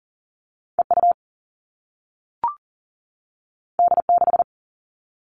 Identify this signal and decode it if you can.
Morse code, heard as a keyed tone, about 42 wpm, 700 Hz. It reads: EU D6
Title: All Cut Numbers - 2-Digit 40wpm